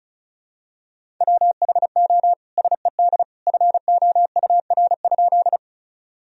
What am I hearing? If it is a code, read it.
Morse code, heard as a keyed tone, about 36 wpm, 700 Hz. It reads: WHO SED FOUR?
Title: Sentences from Top 300 Words and Abbreviations 35wpm